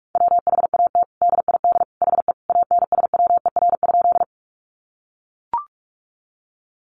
Morse code, heard as a keyed tone, about 44 wpm, 700 Hz. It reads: WHAT DID HE ANSWER?
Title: Sentences from Top 300 Words 45wpm